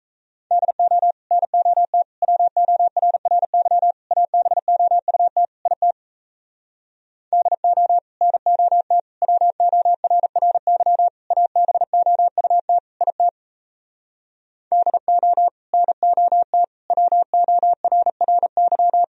Morse code, heard as a keyed tone, about 37 wpm, 700 Hz. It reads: DO NOT WORRY ABOUT IT DO NOT WORRY ABOUT IT DO NOT WORRY